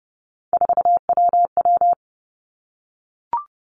Morse code, heard as a keyed tone, about 30 wpm, 700 Hz. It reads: 4WW
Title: Three Challenging Letter-Number Combinations 30wpm